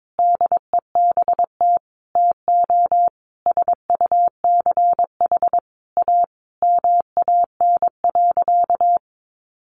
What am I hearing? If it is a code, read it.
Morse code, heard as a keyed tone, about 22 wpm, 700 Hz. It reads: DEBT TO SUCH A MAN.
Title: Sentences from Top 1300 Words 22wpm